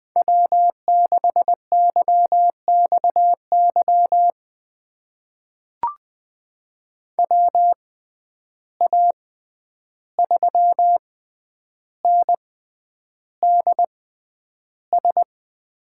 Morse code, heard as a keyed tone, about 20 wpm, spaced as for 10 wpm, 700 Hz. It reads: W6YXY WA3NDS
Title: USA Call Signs - 6x Character Spacing - 20wpm